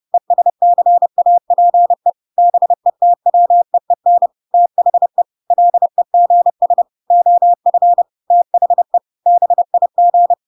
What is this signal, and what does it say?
Morse code, heard as a keyed tone, about 30 wpm, 700 Hz. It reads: ESCAPE BETWEEN THE LEGS OF THE BIG
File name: Pinocchio / Pinocchio ch3 30 30 ik1hge